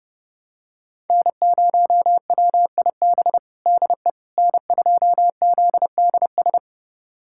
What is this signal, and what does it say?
Morse code, heard as a keyed tone, about 32 wpm, 700 Hz. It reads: N0WIB DE N2ZDS
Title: USA Callsign DE Callsign - 5 Characters - 30wpm